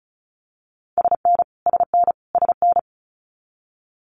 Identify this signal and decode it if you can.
Morse code, heard as a keyed tone, about 34 wpm, 700 Hz. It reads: SN SN SN
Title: Abbreviations Repeated 3x - 35wpm